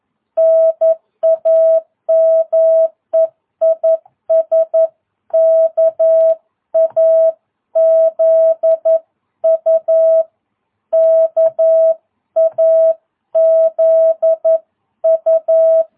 Morse code, heard as a keyed tone, about 11 wpm, 650 Hz. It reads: NAME IS KAZU KAZU
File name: jh8cbh / 2022cw